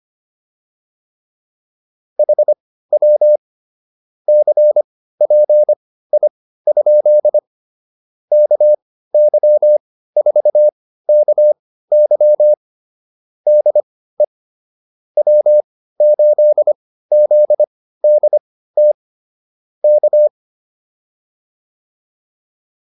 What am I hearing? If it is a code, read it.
Morse code, heard as a keyed tone, about 25 wpm, spaced as for 15 wpm, 600 Hz. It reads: HW CPI? KY4KY DE W8ZDT K